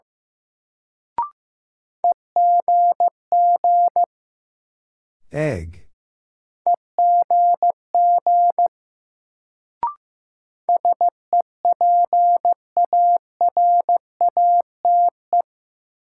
Morse code, heard as a keyed tone, about 15 wpm, 700 Hz. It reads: EGG EGG SEPARATE
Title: Sentences from Top 1000 Words - Review Part 7 - New Word Drill - 15wpm